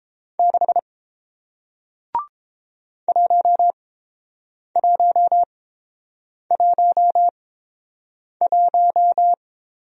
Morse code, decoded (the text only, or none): 6 1111